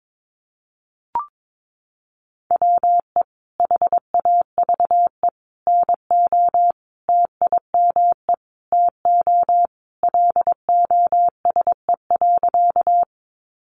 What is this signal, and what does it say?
Morse code, heard as a keyed tone, about 22 wpm, 700 Hz. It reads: WE HAVE NO TIME TO LOSE.
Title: Sentences from Top 1100 Words 22wpm